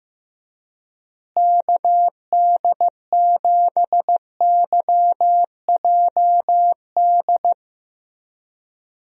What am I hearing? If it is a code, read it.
Morse code, heard as a keyed tone, about 15 wpm, 700 Hz. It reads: KD7YJD
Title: USA Call Signs - 6 Characters - Repeated 2x - 15wpm